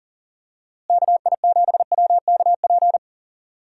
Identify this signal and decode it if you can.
Morse code, heard as a keyed tone, about 40 wpm, 700 Hz. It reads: KI7WKP
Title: USA Call Signs - 6 Characters - Repeated 2x - 40wpm